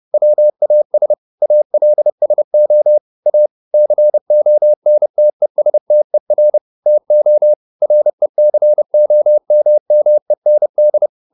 Morse code, heard as a keyed tone, about 30 wpm, 600 Hz. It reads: WAS ALSO A CONTESTER TO RECOMMEND